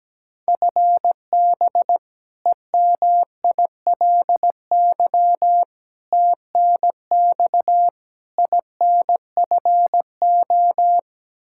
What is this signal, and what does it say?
Morse code, heard as a keyed tone, about 17 wpm, 700 Hz. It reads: FB EMILY TNX INFO
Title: Practice QSO - Exchange 3 - More Info - 17wpm